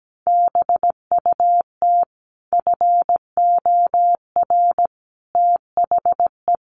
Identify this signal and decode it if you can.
Morse code, heard as a keyed tone, about 17 wpm, 700 Hz. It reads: BUT FOR THE
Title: Sets of 5 Words Encore 17wpm